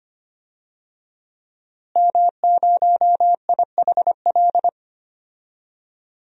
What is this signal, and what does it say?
Morse code, heard as a keyed tone, about 25 wpm, 700 Hz. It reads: M0IHL
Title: DXCC-100 Call Signs - 5 Characters - Repeated 3x - 25wpm